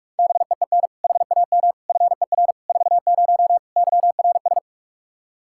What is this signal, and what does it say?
Morse code, heard as a keyed tone, about 42 wpm, 700 Hz. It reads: BEEN HAM FER 40 YRS
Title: Practice QSO - Exchange 3 - More Info - 45wpm